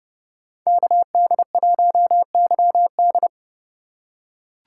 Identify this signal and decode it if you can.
Morse code, heard as a keyed tone, about 32 wpm, 700 Hz. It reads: KD1YD
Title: USA Call Signs - 2x Character Spacing - 30wpm